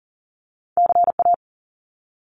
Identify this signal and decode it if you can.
Morse code, heard as a keyed tone, about 40 wpm, 700 Hz. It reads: CA
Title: US States Repeated 3x 40wpm